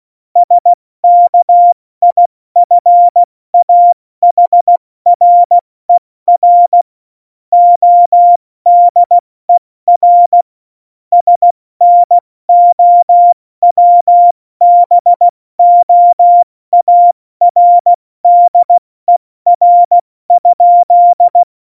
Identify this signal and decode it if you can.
Morse code, heard as a keyed tone, about 16 wpm, 700 Hz. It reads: SKIFAHRER ODER SNOWBOARDER?
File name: Morsekurs / Witze / Witz 28. Januar 2020, 70 F80